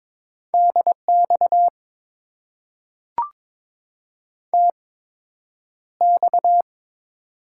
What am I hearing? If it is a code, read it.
Morse code, heard as a keyed tone, about 22 wpm, spaced as for 9 wpm, 700 Hz. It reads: DX TX
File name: Top 90 QSO Elements - 8x Character Spacing - 22wpm